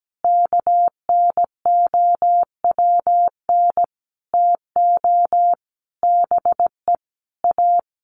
Morse code, heard as a keyed tone, about 17 wpm, 700 Hz. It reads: KNOWN TO BE A
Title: Sentences from Top 1200 Words 17wpm